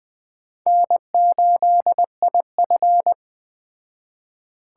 Morse code, heard as a keyed tone, about 20 wpm, 700 Hz. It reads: N8IF
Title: Speed Racing - USA 1x2 Call Signs 17wpm